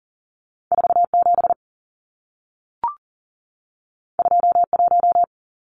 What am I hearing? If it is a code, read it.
Morse code, heard as a keyed tone, about 40 wpm, 700 Hz. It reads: 47 21
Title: Numbers - 2-Digit 40wpm